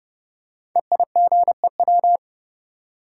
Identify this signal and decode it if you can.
Morse code, heard as a keyed tone, about 31 wpm, 700 Hz. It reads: EIGEW
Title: Code Groups - 5 Letters - 30wpm